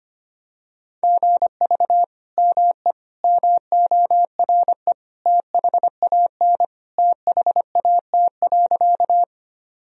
Morse code, heard as a keyed tone, about 25 wpm, 700 Hz. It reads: GV ME MORE THAN THAT.
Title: Sentences from Top 200 Words and Abbreviations 25wpm